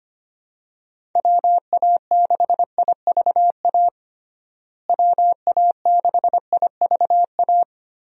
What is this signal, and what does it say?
Morse code, heard as a keyed tone, about 25 wpm, 700 Hz. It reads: WA6IVA WA6IVA
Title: USA Call Signs - 6 Characters - Repeated 3x - 25wpm